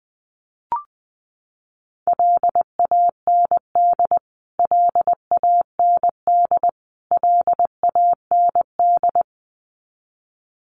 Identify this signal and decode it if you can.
Morse code, heard as a keyed tone, about 20 wpm, 700 Hz. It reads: LAND LAND LAND
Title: Top 200 Words Repeated 20wpm